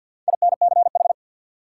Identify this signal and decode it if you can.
Morse code, heard as a keyed tone, about 47 wpm, 700 Hz. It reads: INCH
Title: Sentences from Top 1000 Words - Review Part 3 - New Word Drill - 50wpm